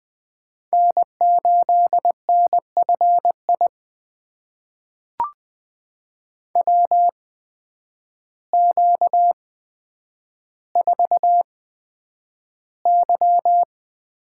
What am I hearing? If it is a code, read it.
Morse code, heard as a keyed tone, about 20 wpm, spaced as for 9 wpm, 700 Hz. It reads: N8NFI WQ4Y
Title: USA Call Signs - 8x Character Spacing - 20wpm